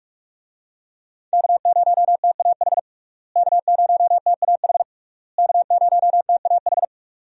Morse code, heard as a keyed tone, about 47 wpm, 700 Hz. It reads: K0TAH K0TAH K0TAH
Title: USA Call Signs - 5 Characters - Repeated 3x - 45wpm